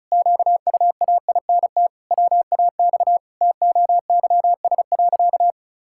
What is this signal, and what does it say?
Morse code, heard as a keyed tone, about 34 wpm, 700 Hz. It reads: QUAINT WAX TOYS.
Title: Pangram Challenge v2 - 35wpm